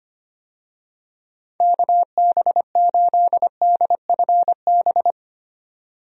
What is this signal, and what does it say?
Morse code, heard as a keyed tone, about 25 wpm, 700 Hz. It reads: KB8DFB
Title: USA Call Signs - 6 Characters - 25wpm